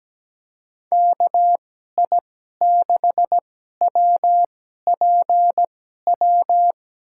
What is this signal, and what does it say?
Morse code, heard as a keyed tone, about 17 wpm, spaced as for 12 wpm, 700 Hz. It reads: KI6WPW
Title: USA Call Signs - 2x Character Spacing - 17wpm